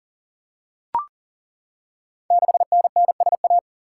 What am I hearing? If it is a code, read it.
Morse code, heard as a keyed tone, about 38 wpm, 690 Hz. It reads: 6NNSA
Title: Code Groups - 5 Letter:Numbers - 40wpm